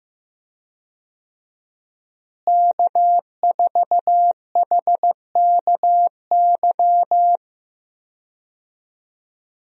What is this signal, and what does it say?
Morse code, heard as a keyed tone, about 15 wpm, 700 Hz. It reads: K4HKY
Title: USA Call Signs - 5 Characters - Repeated 2x - 15wpm